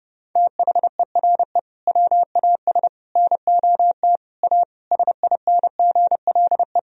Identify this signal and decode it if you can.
Morse code, heard as a keyed tone, about 30 wpm, 700 Hz. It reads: THERE WAS NOT A SINGLE
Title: Sentences from Top 1000 Words - Review Part 4 - 30wpm